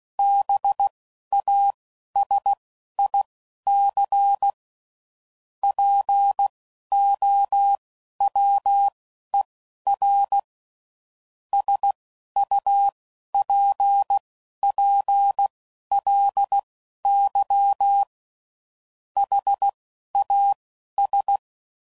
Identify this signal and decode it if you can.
Morse code, heard as a keyed tone, about 16 wpm, spaced as for 12 wpm, 800 Hz. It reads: BASIC POWER SUPPLY HAS
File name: CW Code Groups @10WPM / Technical Subjects / SwPwrSup